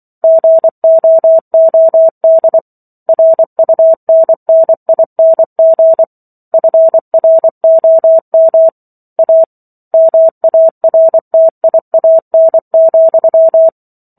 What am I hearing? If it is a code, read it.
Morse code, heard as a keyed tone, about 24 wpm, 650 Hz. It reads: GOOD RUNNING FROM A MARTIAN,